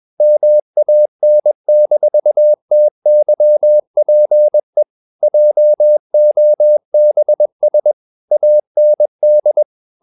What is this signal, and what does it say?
Morse code, heard as a keyed tone, about 21 wpm, 600 Hz. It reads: MAN-TYPE JOBS AND
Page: CW med Gnister